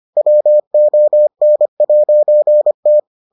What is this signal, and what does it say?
Morse code, heard as a keyed tone, about 25 wpm, 600 Hz. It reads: WON'T